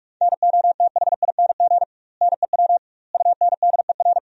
Morse code, heard as a keyed tone, about 46 wpm, 700 Hz. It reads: NOTHING NEW UNDER
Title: 2x Word-Spacing - Sentences from Top 500 Words - 45wpm